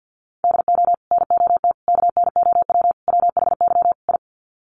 Morse code, heard as a keyed tone, about 50 wpm, 700 Hz. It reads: DO NOT KNOW WHY I
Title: Sets of 5 Words Encore 50wpm